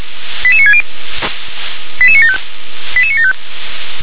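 Five-tone selective call (ZVEI), heard as quick runs of five tones, tones of about 70 ms. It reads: 88960 88975 88975
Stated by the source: ZVEI five-tone selective call received on an SDR.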